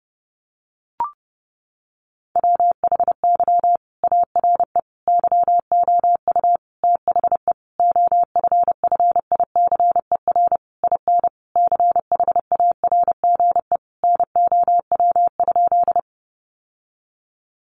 Morse code, heard as a keyed tone, about 30 wpm, 700 Hz. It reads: WHY ARE YOU THE OFFICER IN CHARGE NOW?
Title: Sentences from Top 1200 Words 30wpm